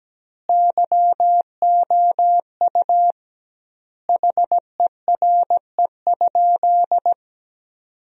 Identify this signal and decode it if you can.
Morse code, heard as a keyed tone, about 17 wpm, 700 Hz. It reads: YOU HERE?
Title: 2x Word-Spacing - Sentences from Top 200 Words - 17wpm